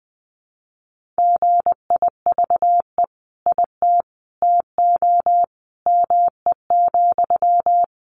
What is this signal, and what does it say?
Morse code, heard as a keyed tone, about 20 wpm, 700 Hz. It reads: GIVE IT TO ME,
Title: Sentences from Top 200 Words 20wpm